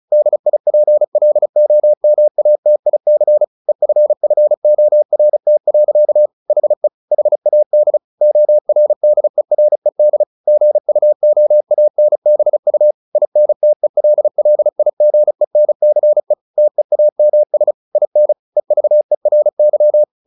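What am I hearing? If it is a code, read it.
Morse code, heard as a keyed tone, about 34 wpm, 600 Hz. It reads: DIPLOMATIC EFFORT. HE HAD ORDERED GUOANBU INTELLIGENCE TEAMS IN EVERY
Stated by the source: Morse practice at 35 wpm from CW med Gnister, sending Ken Follett's novel Never in English.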